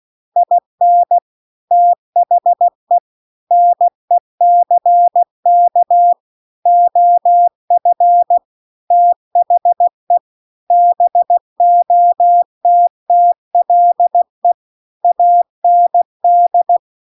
Morse code, heard as a keyed tone, about 16 wpm, 700 Hz. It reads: IN THE NECK OF THE BOTTLE AND